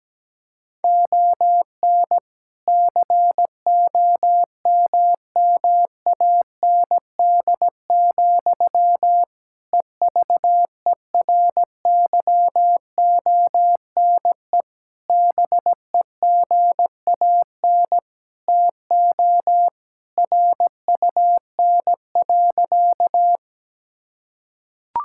Morse code, heard as a keyed tone, about 17 wpm, 700 Hz. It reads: ON COMMAND, EVERYONE BEGAN TO RUN.
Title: Sentences from Top 1600 Words 17wpm